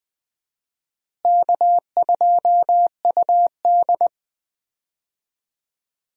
Morse code, heard as a keyed tone, about 21 wpm, 700 Hz. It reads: K2UD
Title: USA Call Signs - 4 Characters - Repeated 2x - 20wpm